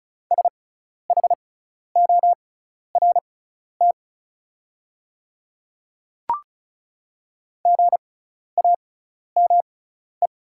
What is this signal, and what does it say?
Morse code, heard as a keyed tone, about 36 wpm, spaced as for 12 wpm, 690 Hz. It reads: SHORT GAME
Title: 6x Character Spacing - Top 500 Words - 35wpm